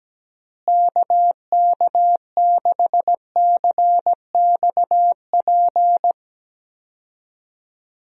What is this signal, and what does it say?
Morse code, heard as a keyed tone, about 17 wpm, 700 Hz. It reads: KK6CXP
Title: USA Call Signs - 6 Characters - Repeated 2x - 17wpm